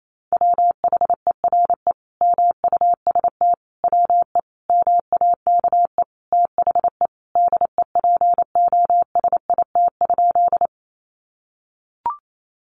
Morse code, heard as a keyed tone, about 28 wpm, 700 Hz. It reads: WHERE MUST WE MAKE THE DEPOSIT?
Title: Sentences from Top 1300 Words 28wpm